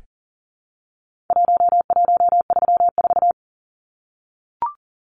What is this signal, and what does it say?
Morse code, heard as a keyed tone, about 40 wpm, 700 Hz. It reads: 1134
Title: Numbers - Contest Style - 4-Digit 40wpm